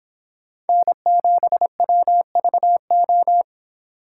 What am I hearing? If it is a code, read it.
Morse code, heard as a keyed tone, about 26 wpm, 700 Hz. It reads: N7WVO